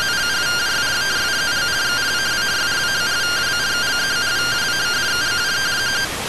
Two-tone teleprinter signal, about 170 Hz shift.